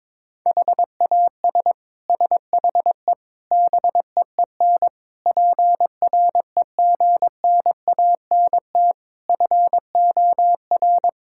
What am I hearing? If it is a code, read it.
Morse code, heard as a keyed tone, about 22 wpm, 700 Hz. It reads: HAS SHE BEEN PREGNANT FOR
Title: Sentences from Top 1000 Words - Review Part 6 - 22wpm